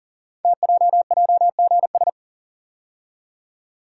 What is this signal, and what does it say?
Morse code, heard as a keyed tone, about 39 wpm, 700 Hz. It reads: TJJGS
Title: Code Groups - 5 Letters - 40wpm